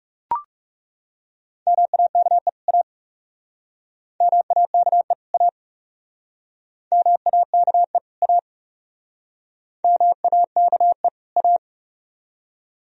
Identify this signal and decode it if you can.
Morse code, heard as a keyed tone, about 39 wpm, 700 Hz. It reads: MAKE A MAKE A MAKE A MAKE A